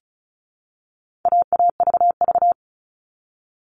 Morse code, heard as a keyed tone, about 34 wpm, 700 Hz. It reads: AAVV
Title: All Cut Numbers - 4-Digit 35wpm